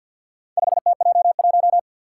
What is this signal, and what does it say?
Morse code, heard as a keyed tone, about 52 wpm, 700 Hz. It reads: 5TJ1
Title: Four Challenging Letter-Number Combinations 50wpm